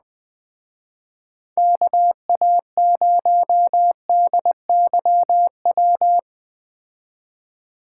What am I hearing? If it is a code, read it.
Morse code, heard as a keyed tone, about 20 wpm, 700 Hz. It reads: KA0DYW
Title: Speed Racing - USA Call Signs 17wpm